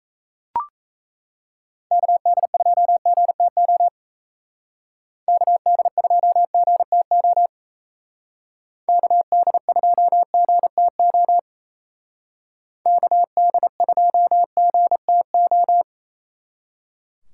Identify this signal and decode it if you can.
Morse code, heard as a keyed tone, about 36 wpm, 700 Hz. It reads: KD2GTO KD2GTO KD2GTO KD2GTO